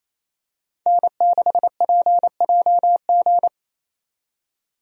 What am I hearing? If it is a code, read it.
Morse code, heard as a keyed tone, about 28 wpm, 700 Hz. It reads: N6PJG
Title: USA Call Signs - 5 Characters - 28wpm